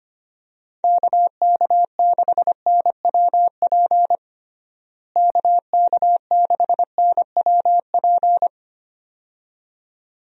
Morse code, heard as a keyed tone, about 25 wpm, 700 Hz. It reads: KK6NWP KK6NWP
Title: USA Call Signs - 6 Characters - Repeated 2x - 25wpm